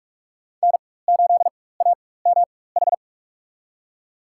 Morse code, heard as a keyed tone, about 46 wpm, spaced as for 21 wpm, 700 Hz. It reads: N8AMH
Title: USA Call Signs - 4x Character Spacing - 45wpm